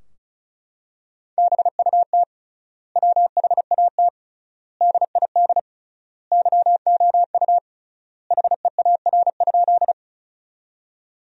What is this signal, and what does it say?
Morse code, heard as a keyed tone, about 36 wpm, 700 Hz. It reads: BUT WHAT DID YOU HEAR?
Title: 3x Word-Spacing - Sentences from Top 300 Words - 35wpm